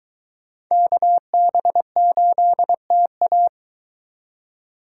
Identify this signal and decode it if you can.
Morse code, heard as a keyed tone, about 22 wpm, 700 Hz. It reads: KB8TA